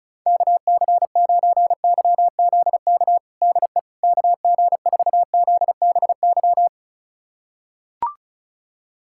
Morse code, heard as a keyed tone, about 34 wpm, 700 Hz. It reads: KC9YZK DE KG4ZBY
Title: USA Callsign DE Callsign 35wpm